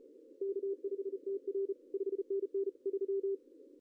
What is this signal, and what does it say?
Morse code, heard as a keyed tone, about 35 wpm, 410 Hz. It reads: K5TR 5NN 3
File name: CW / cqww-cw-k5tr